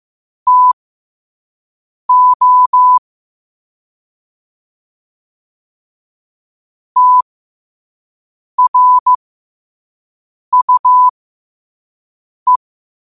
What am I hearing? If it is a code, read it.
Morse code, heard as a keyed tone, about 15 wpm, spaced as for 5 wpm, 1000 Hz. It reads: TO TRUE
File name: qotdfiles / 5wpm